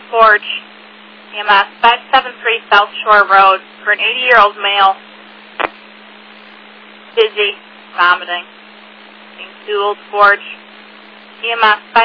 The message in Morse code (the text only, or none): none